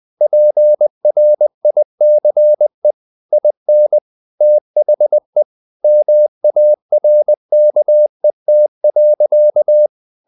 Morse code, heard as a keyed tone, about 20 wpm, 600 Hz. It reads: PRICE IN THE MARKET.
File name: the old man and the sea part 2 20wpm